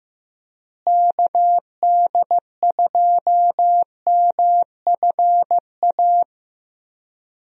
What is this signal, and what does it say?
Morse code, heard as a keyed tone, about 15 wpm, 700 Hz. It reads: KD2MFA
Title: USA Call Signs - 6 Characters - Repeated 3x - 15wpm